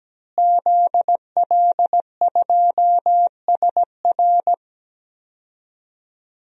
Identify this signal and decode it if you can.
Morse code, heard as a keyed tone, about 17 wpm, 700 Hz. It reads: ZL2SR
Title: DXCC-100 Call Signs - 5 Characters - Repeated 3x - 17wpm